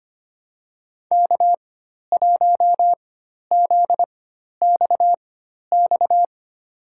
Morse code, heard as a keyed tone, about 25 wpm, spaced as for 12 wpm, 700 Hz. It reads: K1ZXX
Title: USA Call Signs - 4x Character Spacing - 25wpm